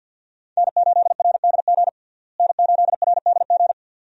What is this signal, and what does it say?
Morse code, heard as a keyed tone, about 52 wpm, 700 Hz. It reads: N8RDG N8RDG
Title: USA Call Signs - 5 Characters - Repeated 2x - 50wpm